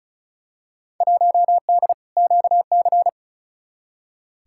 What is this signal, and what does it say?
Morse code, heard as a keyed tone, about 36 wpm, 700 Hz. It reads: 1D QC
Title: Field Day Exchanges Encore - 35wpm